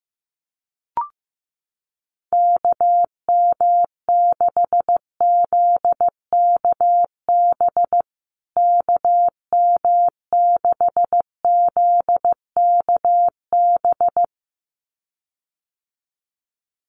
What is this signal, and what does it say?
Morse code, heard as a keyed tone, about 15 wpm, 700 Hz. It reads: KM6ZKB KM6ZKB
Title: USA Call Signs Repeated 2x 15wpm